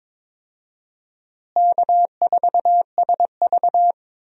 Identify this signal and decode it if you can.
Morse code, heard as a keyed tone, about 22 wpm, 700 Hz. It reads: K4SV
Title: USA Call Signs - 4 Characters - Repeated 2x - 22wpm